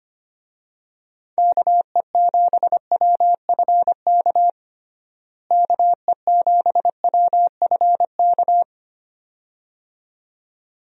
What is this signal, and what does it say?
Morse code, heard as a keyed tone, about 25 wpm, 700 Hz. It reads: KE7WFK KE7WFK
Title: USA Call Signs - 6 Characters - Repeated 2x - 25wpm